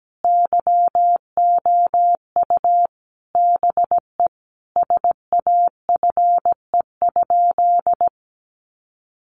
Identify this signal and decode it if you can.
Morse code, heard as a keyed tone, about 17 wpm, 700 Hz. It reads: YOU BE SAFE?
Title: Sentences from Top 900 Words 17wpm